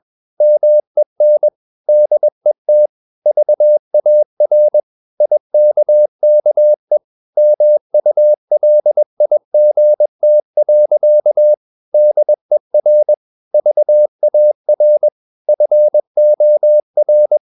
Morse code, heard as a keyed tone, about 21 wpm, 600 Hz. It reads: MEN DET VAR IKKE MULIGT. DER VAR FOR
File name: Ferie QRS 21wpm